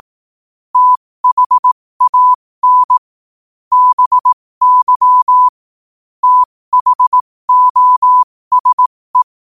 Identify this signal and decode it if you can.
Morse code, heard as a keyed tone, about 18 wpm, spaced as for 15 wpm, 1000 Hz. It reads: THAN BY THOSE
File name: qotdfiles / 15wpm